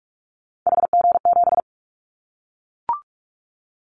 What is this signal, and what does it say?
Morse code, heard as a keyed tone, about 44 wpm, 700 Hz. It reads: HG7
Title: Callsign Prefixes for Top 25 DXCC - Repeated 3x - 45wpm